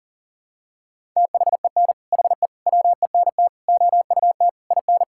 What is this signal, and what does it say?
Morse code, heard as a keyed tone, about 40 wpm, 700 Hz. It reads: THEN HE WENT OUT IN
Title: Sentences from Top 1000 Words - Review Part 5 - 40wpm